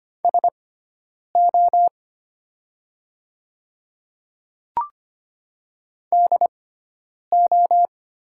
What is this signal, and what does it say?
Morse code, heard as a keyed tone, about 25 wpm, spaced as for 9 wpm, 700 Hz. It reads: SO DO
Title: 6x Character Spacing - Top 100 Words - 25wpm